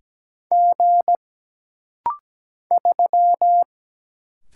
Morse code, heard as a keyed tone, about 17 wpm, 700 Hz. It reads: G 3
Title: Single Letter-Number Alt - Rapid-Fire 17wpm